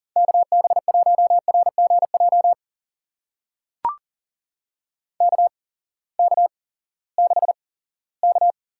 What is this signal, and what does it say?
Morse code, heard as a keyed tone, about 39 wpm, spaced as for 20 wpm, 700 Hz. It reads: KB1RGJ KK6K
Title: USA Call Signs - 6 Characters - 8x Character Spacing - 40wpm